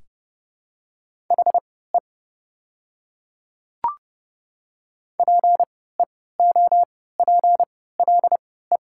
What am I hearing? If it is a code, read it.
Morse code, heard as a keyed tone, about 30 wpm, spaced as for 17 wpm, 700 Hz. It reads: HE PEOPLE
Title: 3x Character Spacing - Top 100 Words - 30wpm